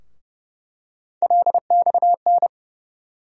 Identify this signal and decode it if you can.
Morse code, heard as a keyed tone, about 30 wpm, 700 Hz. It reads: LXN
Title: USA Call Sign Postfixes - 3 Letters 30wpm